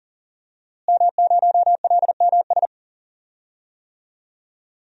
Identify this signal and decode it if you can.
Morse code, heard as a keyed tone, about 38 wpm, 700 Hz. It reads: M0LMS
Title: DXCC-100 Call Signs - 5 Characters - Repeated 2x - 40wpm